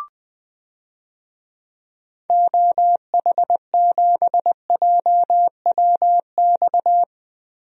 Morse code, heard as a keyed tone, about 20 wpm, 700 Hz. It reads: OH7JWX
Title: DXCC-100 Call Signs - 6 Characters - 20wpm